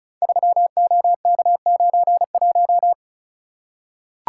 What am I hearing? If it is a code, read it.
Morse code, heard as a keyed tone, about 34 wpm, 700 Hz. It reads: 3OK91
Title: Code Groups - 5 Letter:Numbers - 35wpm